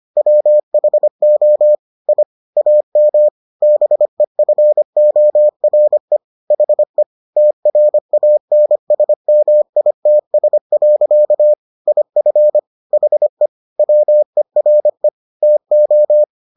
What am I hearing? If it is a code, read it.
Morse code, heard as a keyed tone, about 25 wpm, 600 Hz. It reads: WHO I AM BEFORE HE TRANSMITS. IF HE WERE TO